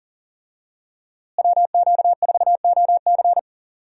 Morse code, heard as a keyed tone, about 40 wpm, 700 Hz. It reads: WQ4OC
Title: USA Call Signs - 5 Characters - Repeated 3x - 40wpm